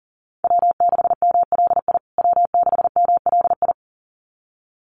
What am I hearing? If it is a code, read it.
Morse code, heard as a keyed tone, about 40 wpm, 700 Hz. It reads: W6MLI W6MLI
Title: USA Call Signs Repeated 2x 40wpm